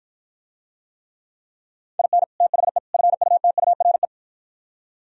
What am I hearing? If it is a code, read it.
Morse code, heard as a keyed tone, about 50 wpm, 700 Hz. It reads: IN THE FUTURE